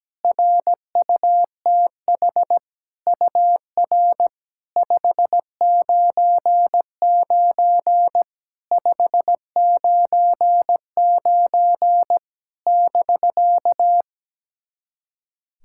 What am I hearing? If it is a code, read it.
Morse code, heard as a keyed tone, about 17 wpm, 700 Hz. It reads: RUTH UR 599 599 <BK>
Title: SOTA QSO Practice 17wpm